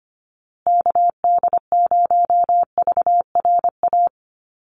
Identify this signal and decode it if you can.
Morse code, heard as a keyed tone, about 25 wpm, 700 Hz. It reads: KD0VRA
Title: USA Call Signs Repeated 2x 25wpm